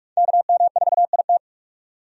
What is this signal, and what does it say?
Morse code, heard as a keyed tone, about 42 wpm, 700 Hz. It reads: KM4IT
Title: USA Call Signs - 5 Characters - 45wpm